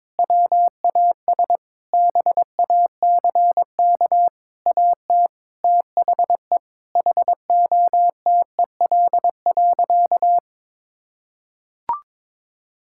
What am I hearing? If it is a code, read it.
Morse code, heard as a keyed tone, about 21 wpm, 700 Hz. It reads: WAS BACK AT THE HOTEL.
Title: Sentences from Top 1600 Words 22wpm